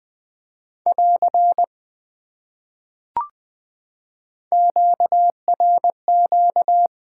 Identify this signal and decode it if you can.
Morse code, heard as a keyed tone, about 20 wpm, 700 Hz. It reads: <AR> QRQ
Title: Top 90 QSO Elements 20wpm